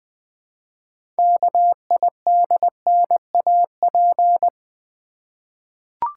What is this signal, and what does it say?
Morse code, heard as a keyed tone, about 20 wpm, 700 Hz. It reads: KIDNAP
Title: 6 Letter Words - First 3 Letters Given - 20wpm